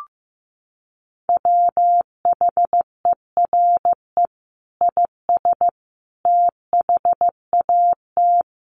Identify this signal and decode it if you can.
Morse code, heard as a keyed tone, about 15 wpm, 700 Hz. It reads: WHERE IS THAT
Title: Sentences from Top 1500 Words 15wpm